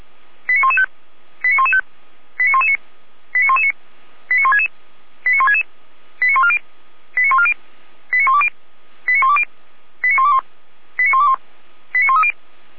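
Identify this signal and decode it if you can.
Five-tone selective call (ZVEI), heard as quick runs of five tones, tones of about 80 ms. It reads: 89115 89115 89119 89119 89166 89166 89140 89140 89120 89120 89121 89121 89130
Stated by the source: ZVEI five-tone selective call 89078, received on an SDR.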